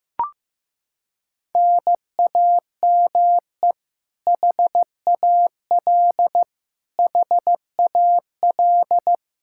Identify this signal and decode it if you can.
Morse code, heard as a keyed tone, about 15 wpm, 700 Hz. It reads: NAME HAL HAL
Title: Practice QSO - Exchange 1 - Beginning - 15wpm